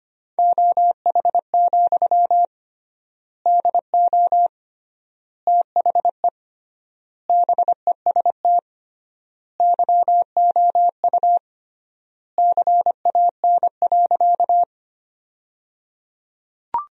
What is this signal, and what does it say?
Morse code, heard as a keyed tone, about 25 wpm, 700 Hz. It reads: OH, DO THE BEST YOU CAN.
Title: 3x Word-Spacing - Sentences from Top 500 Words - 25wpm